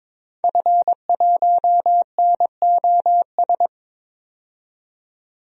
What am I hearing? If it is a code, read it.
Morse code, heard as a keyed tone, about 21 wpm, 700 Hz. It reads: F1NOS
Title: DXCC-100 Call Signs - 5 Characters - 22wpm